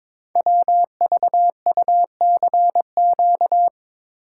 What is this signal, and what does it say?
Morse code, heard as a keyed tone, about 22 wpm, 700 Hz. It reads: WVUCQ
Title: Code Groups - 5 Letters - 22wpm